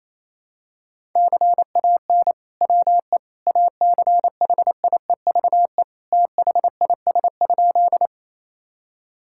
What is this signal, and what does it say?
Morse code, heard as a keyed tone, about 28 wpm, 700 Hz. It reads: CAN WE ACHIEVE THIS?
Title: Sentences from Top 1000 Words - Review Part 6 - 28wpm